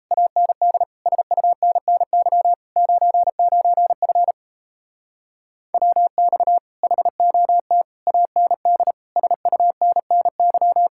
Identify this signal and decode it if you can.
Morse code, heard as a keyed tone, about 37 wpm, 700 Hz. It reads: AND SUNNY 99F WX HOT AND SUNNY